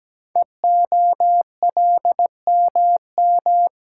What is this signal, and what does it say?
Morse code, heard as a keyed tone, about 17 wpm, 700 Hz. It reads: EOLMM
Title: Code Groups - 5 Letters - 17wpm